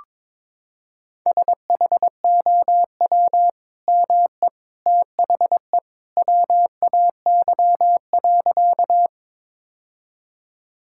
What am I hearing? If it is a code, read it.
Morse code, heard as a keyed tone, about 22 wpm, 700 Hz. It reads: SHOW ME THE WAY.
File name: practice / 22wpm / Sentences with 2-Word Phrases - 22wpm